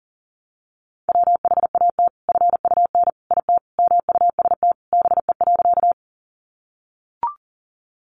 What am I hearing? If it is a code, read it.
Morse code, heard as a keyed tone, about 40 wpm, 700 Hz. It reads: WHAT FUN IT MUST BE.
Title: Sentences from Top 700 Words 40wpm